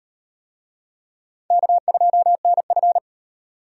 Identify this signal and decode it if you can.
Morse code, heard as a keyed tone, about 37 wpm, 700 Hz. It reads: K2NF